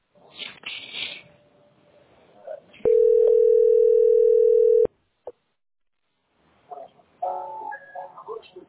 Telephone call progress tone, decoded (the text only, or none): ringback tone